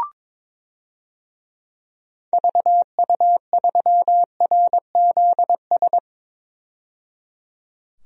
Morse code, heard as a keyed tone, about 22 wpm, 700 Hz. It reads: VU3RZS
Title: DXCC-100 Call Signs - 6 Characters - 22wpm